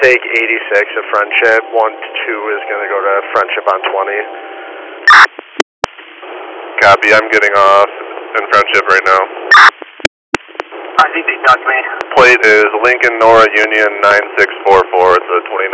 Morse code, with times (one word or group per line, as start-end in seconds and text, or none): none